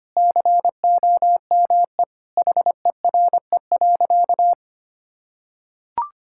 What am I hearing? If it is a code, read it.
Morse code, heard as a keyed tone, about 25 wpm, 700 Hz. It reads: COME HERE.
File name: practice / 25wpm / Sentences with 2-Word Phrases - 25wpm